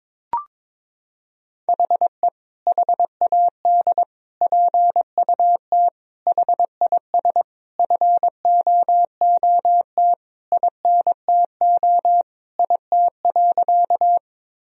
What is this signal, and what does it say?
Morse code, heard as a keyed tone, about 23 wpm, 700 Hz. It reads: HE HAD PUT HIS FOOT INTO IT.
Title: Sentences from Top 1000 Words - Review Part 3 - 22wpm